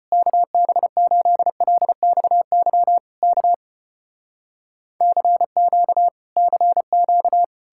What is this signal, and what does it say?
Morse code, heard as a keyed tone, about 34 wpm, 700 Hz. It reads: KB8LXY K CQ CQ